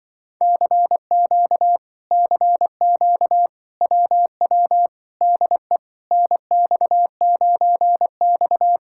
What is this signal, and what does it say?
Morse code, heard as a keyed tone, about 24 wpm, 700 Hz. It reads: CQ CQ WW DE NX9X
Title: Speed-Racing Common QSO Phrases - 20wpm